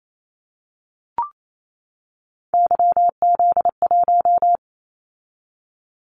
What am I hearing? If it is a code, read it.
Morse code, heard as a keyed tone, about 28 wpm, 700 Hz. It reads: YZ1
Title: Callsign Prefixes for Top 25 DXCC - 28wpm